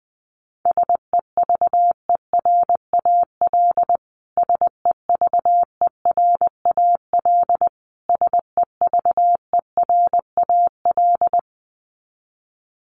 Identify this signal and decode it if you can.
Morse code, heard as a keyed tone, about 20 wpm, 700 Hz. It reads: SEVERAL SEVERAL SEVERAL
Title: Top 500 Words Repeated 20wpm